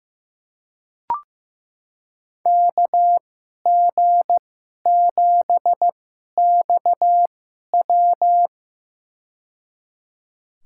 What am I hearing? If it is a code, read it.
Morse code, heard as a keyed tone, about 15 wpm, spaced as for 11 wpm, 700 Hz. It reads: KG7XW
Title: USA Call Signs - 5 Characters - 2x Character Spacing - 15wpm